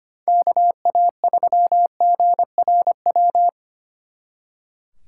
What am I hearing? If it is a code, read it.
Morse code, heard as a keyed tone, about 31 wpm, 700 Hz. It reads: KA3GRW
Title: Speed Racing - USA 2x3 Call Signs 25wpm